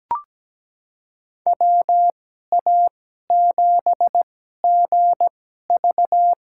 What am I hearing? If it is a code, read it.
Morse code, heard as a keyed tone, about 17 wpm, spaced as for 12 wpm, 700 Hz. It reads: WA7GV
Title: USA Call Signs - 2x Character Spacing - 17wpm